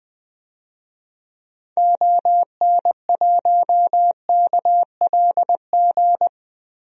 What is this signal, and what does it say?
Morse code, heard as a keyed tone, about 20 wpm, 700 Hz. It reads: ON1KLG
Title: DXCC-100 Call Signs - 6 Characters - Repeated 3x - 20wpm